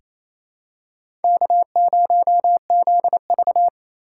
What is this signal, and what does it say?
Morse code, heard as a keyed tone, about 28 wpm, 700 Hz. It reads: K0ZV
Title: USA Call Signs - 4 Characters - 2x Character Spacing - 28wpm